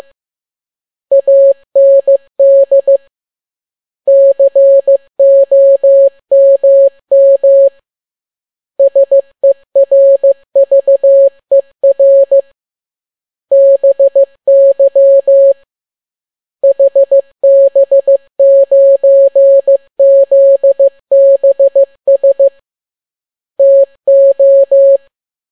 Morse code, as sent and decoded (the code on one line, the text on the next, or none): .- -. -.. / -.-. --- -- -- / ... .-. ...- . .-. / -... -.-- / .... -... ----. --.. -... ... / - ---
AND COMM SERVER BY HB9ZBS TO